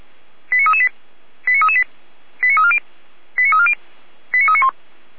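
Five-tone selective call (ZVEI), heard as quick runs of five tones, tones of about 70 ms. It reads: 89338 89338 89340 89340 89391
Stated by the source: ZVEI five-tone call 89310 received on an SDR.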